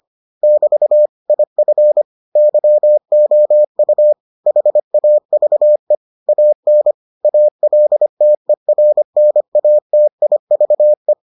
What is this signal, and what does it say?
Morse code, heard as a keyed tone, about 25 wpm, 600 Hz. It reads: <BT> IF YOU HAVE AN ALTERNATIVE